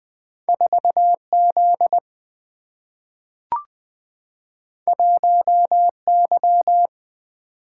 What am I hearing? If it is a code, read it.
Morse code, heard as a keyed tone, about 20 wpm, 700 Hz. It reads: 4 Z 1 Y
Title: Single-Letter Single-Number 20wpm